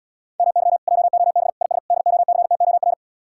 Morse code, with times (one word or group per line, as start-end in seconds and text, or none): 0.4-0.8 MY
0.9-1.5 JOB
1.6-1.8 IS
1.9-2.9 DOCTOR